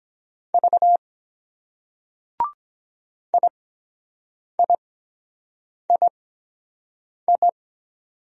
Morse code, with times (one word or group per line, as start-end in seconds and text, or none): 0.5-1.0 V
3.3-7.5 IIII